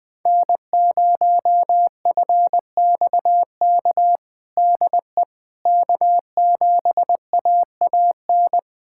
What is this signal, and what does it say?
Morse code, heard as a keyed tone, about 20 wpm, 700 Hz. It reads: N0FXK DE K7AAN
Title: USA Callsign DE Callsign - 5 Characters - 20wpm